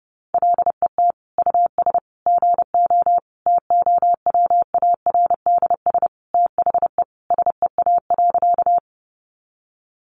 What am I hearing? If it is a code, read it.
Morse code, heard as a keyed tone, about 30 wpm, 700 Hz. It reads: LET US GO TOWARDS THE SEA.
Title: Sentences from Top 1100 Words 30wpm